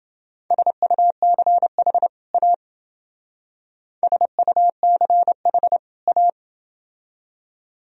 Morse code, heard as a keyed tone, about 28 wpm, 700 Hz. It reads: SUCH A SUCH A